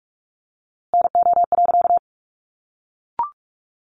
Morse code, heard as a keyed tone, about 44 wpm, 700 Hz. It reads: NO.
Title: A Sound Of Thunder 45wpm